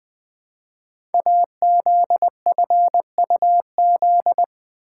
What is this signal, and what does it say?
Morse code, heard as a keyed tone, about 20 wpm, 700 Hz. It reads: AZFUZ
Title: Code Groups - 5 Letters - 20wpm